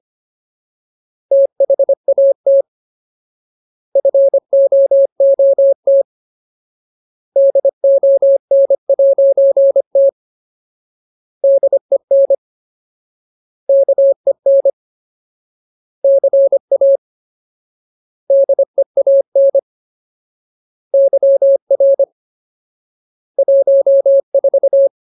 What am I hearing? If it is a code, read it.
Morse code, heard as a keyed tone, about 25 wpm, 550 Hz. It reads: THAT FOOT DON'T DEN KEN CA DEAN YR 14
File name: Ultimate Word List / 25WPM Ultimate